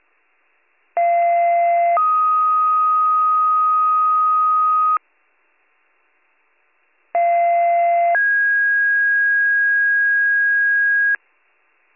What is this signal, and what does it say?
Two-tone page, heard as a one-second tone, then a three-second tone, about 690 then 1250 Hz, about 690 then 1650 Hz, with noise. It is heard from 1.0 to 11.2 s.